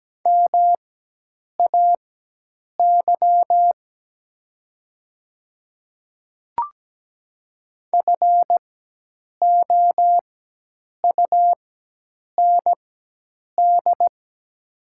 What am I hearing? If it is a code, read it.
Morse code, heard as a keyed tone, about 17 wpm, spaced as for 8 wpm, 700 Hz. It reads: MAY FOUND
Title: 4x Character Spacing - Top 300 Words - 17wpm